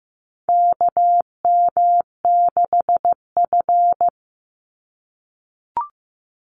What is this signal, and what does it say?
Morse code, heard as a keyed tone, about 15 wpm, 700 Hz. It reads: KM6F
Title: USA Call Signs 15wpm